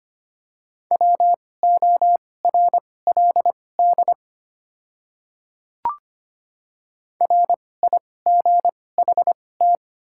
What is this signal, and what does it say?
Morse code, heard as a keyed tone, about 25 wpm, spaced as for 18 wpm, 700 Hz. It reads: WORLD RIGHT
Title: 2x Character Spacing - Top 200 Words - 25wpm